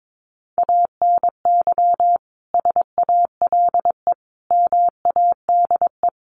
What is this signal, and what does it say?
Morse code, heard as a keyed tone, about 22 wpm, 700 Hz. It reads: ANY SALE MADE
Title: Sentences from Top 1400 Words 22wpm